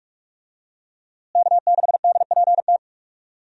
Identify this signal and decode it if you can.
Morse code, heard as a keyed tone, about 47 wpm, 700 Hz. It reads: K6DPT
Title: USA Call Signs - 5 Characters - Repeated 2x - 45wpm